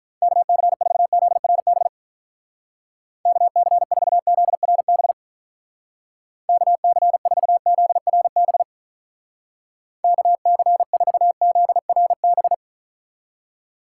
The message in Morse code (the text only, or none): KC4ZRB KC4ZRB KC4ZRB KC4ZRB